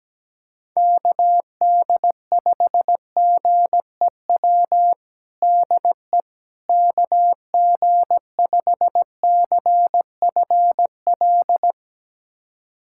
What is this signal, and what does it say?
Morse code, heard as a keyed tone, about 17 wpm, 700 Hz. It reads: KD5GEW DE KG5CFL
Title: USA Callsign DE Callsign - 6 Characters - 17wpm